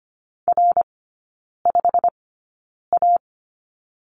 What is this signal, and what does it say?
Morse code, heard as a keyed tone, about 25 wpm, spaced as for 9 wpm, 690 Hz. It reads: R5A
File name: Lesson 15e - Number 5 - Three Characters 25wpm